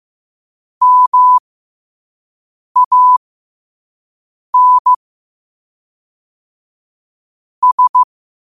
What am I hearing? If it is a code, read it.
Morse code, heard as a keyed tone, about 15 wpm, spaced as for 5 wpm, 1000 Hz. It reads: MAN S
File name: qotdfiles / 5wpm